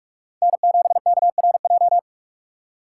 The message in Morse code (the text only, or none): N7KRJ